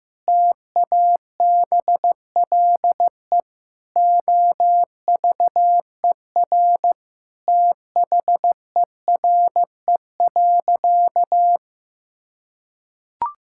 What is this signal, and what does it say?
Morse code, heard as a keyed tone, about 16 wpm, 700 Hz. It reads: TABLE OVER THERE.
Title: Sentences from Top 1000 Words - Review Part 6 - 15wpm